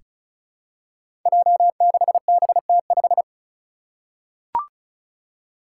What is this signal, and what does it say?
Morse code, heard as a keyed tone, about 34 wpm, 700 Hz. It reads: J6BT5
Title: Five Challenging Letter-Number Combinations 35wpm